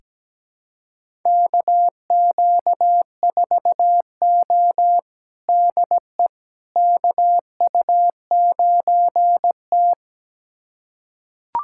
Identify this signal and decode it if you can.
Morse code, heard as a keyed tone, about 17 wpm, 700 Hz. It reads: KQ4O DE KU9T
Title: USA Callsign DE Callsign - 4 Characters - 17wpm